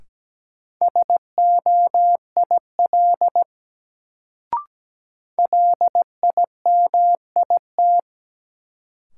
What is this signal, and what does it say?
Morse code, heard as a keyed tone, about 17 wpm, 700 Hz. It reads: SOIL LIMIT
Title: Sentences from Top 1500 Words - New Word Drill Alt - 17wpm